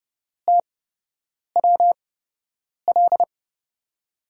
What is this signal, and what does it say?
Morse code, heard as a keyed tone, about 30 wpm, spaced as for 8 wpm, 700 Hz. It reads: TWL
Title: USA Call Signs - 6 Characters - 8x Character Spacing - 30wpm